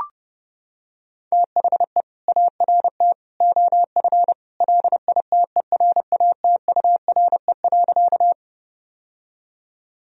Morse code, heard as a keyed tone, about 32 wpm, 700 Hz. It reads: THE ART OF LITERATURE.
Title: Sentences from Top 1000 Words - Review Part 3 - 30wpm